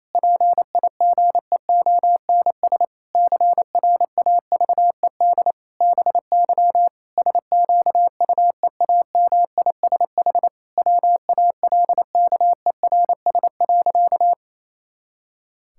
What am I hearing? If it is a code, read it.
Morse code, heard as a keyed tone, about 27 wpm, 700 Hz. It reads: PIGEONS CRAVED BY SQUEAMISH WALKERS.
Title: Pangram Challenge v2 - 28wpm